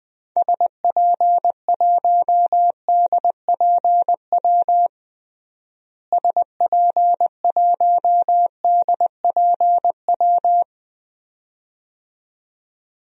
Morse code, heard as a keyed tone, about 21 wpm, 700 Hz. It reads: SP1DPW SP1DPW
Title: DXCC-100 Call Signs - 6 Characters - Repeated 2x - 20wpm